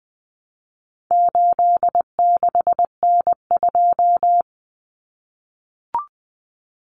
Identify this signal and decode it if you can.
Morse code, heard as a keyed tone, about 20 wpm, 700 Hz. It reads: 86N2
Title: Common Cut Numbers - 4-Digit 20wpm